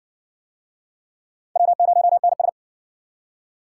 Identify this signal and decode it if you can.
Morse code, heard as a keyed tone, about 56 wpm, 700 Hz. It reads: W0NS